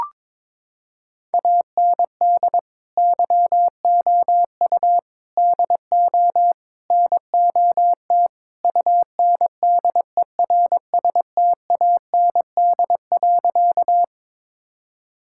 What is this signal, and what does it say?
Morse code, heard as a keyed tone, about 22 wpm, 700 Hz. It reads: AND YOU DO NOT UNDERSTAND.
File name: practice / 22wpm / Sentences with 4-Word Phrases - 22wpm